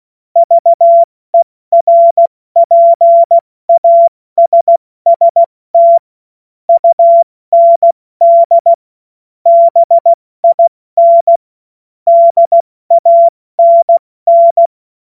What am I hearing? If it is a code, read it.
Morse code, heard as a keyed tone, about 16 wpm, 680 Hz. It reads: VERPASST UND BIN DANN
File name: Morsekurs / Witze / Witz 06. April 2021, 70 F80